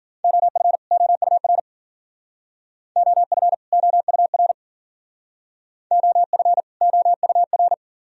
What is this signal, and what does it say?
Morse code, heard as a keyed tone, about 48 wpm, 700 Hz. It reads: OF OUR OF OUR OF OUR
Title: Speed-Racing - Sets of 2 Words 40wpm